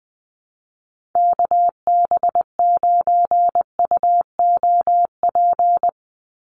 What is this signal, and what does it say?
Morse code, heard as a keyed tone, about 20 wpm, 700 Hz. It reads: KB9UOP
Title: USA Call Signs 20wpm